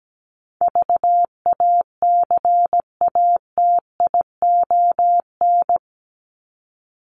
Morse code, heard as a keyed tone, about 17 wpm, 700 Hz. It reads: VACATION
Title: Top 1501-1750 Words 17wpm